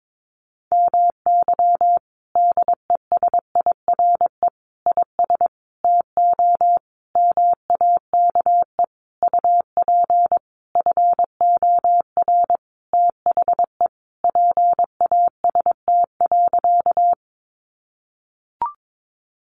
Morse code, heard as a keyed tone, about 22 wpm, 700 Hz. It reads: MY DESIRE IS TO MAKE UP FOR THE PAST.
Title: Sentences from Top 1100 Words 22wpm